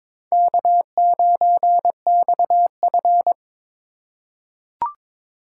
Morse code, heard as a keyed tone, about 22 wpm, 700 Hz. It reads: K9XF
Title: USA Call Signs - 4 Characters - 2x Character Spacing - 22wpm